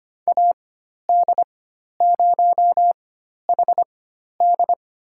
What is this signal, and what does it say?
Morse code, heard as a keyed tone, about 25 wpm, spaced as for 12 wpm, 700 Hz. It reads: AD0HD
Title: USA Call Signs - 5 Characters - 4x Character Spacing - 25wpm